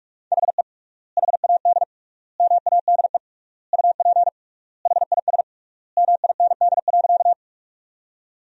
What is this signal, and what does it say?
Morse code, heard as a keyed tone, about 41 wpm, 700 Hz. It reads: HE HAD MADE UP HIS MIND.
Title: 3x Word-Spacing - Sentences from Top 500 Words - 45wpm